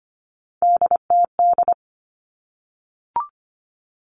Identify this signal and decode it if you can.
Morse code, heard as a keyed tone, about 25 wpm, 700 Hz. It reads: DTD
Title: All Cut Numbers - 3-Digit 25wpm